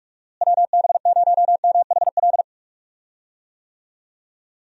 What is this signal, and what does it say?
Morse code, heard as a keyed tone, about 45 wpm, 700 Hz. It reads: WB0MHL
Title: USA Call Signs - 6 Characters - Repeated 2x - 45wpm